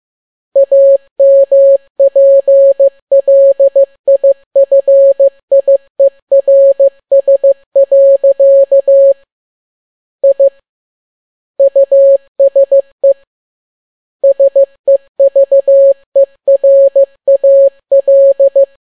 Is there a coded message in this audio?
Yes: Morse code from 0.6 to 18.6 s, heard as a keyed tone, about 15 wpm, 550 Hz. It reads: AMPLIFIERS. I USE SEVERAL